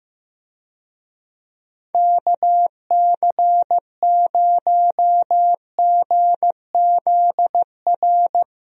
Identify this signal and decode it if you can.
Morse code, heard as a keyed tone, about 15 wpm, 700 Hz. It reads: KC0GZR
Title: USA Call Signs - 6 Characters - Repeated 2x - 15wpm